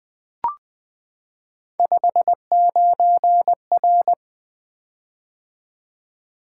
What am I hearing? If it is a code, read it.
Morse code, heard as a keyed tone, about 20 wpm, 700 Hz. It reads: 59R
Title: Three Letter-Number Combinations - 20wpm